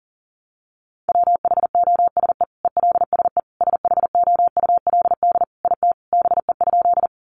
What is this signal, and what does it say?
Morse code, heard as a keyed tone, about 40 wpm, 700 Hz. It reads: WHOSE ELSE SHOULD IT BE?
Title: Sentences from Top 1200 Words 40wpm